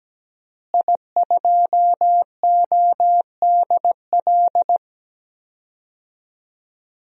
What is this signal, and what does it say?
Morse code, heard as a keyed tone, about 17 wpm, 700 Hz. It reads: I2ODL
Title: DXCC-100 Call Signs - 5 Characters - Repeated 3x - 17wpm